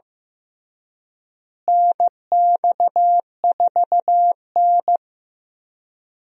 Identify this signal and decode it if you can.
Morse code, heard as a keyed tone, about 15 wpm, 700 Hz. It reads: NX4N
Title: USA Call Signs - 4 Characters - Repeated 3x - 15wpm